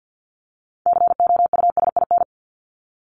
Morse code, heard as a keyed tone, about 50 wpm, 700 Hz. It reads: COUSIN
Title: Top 1501-1750 Words 50wpm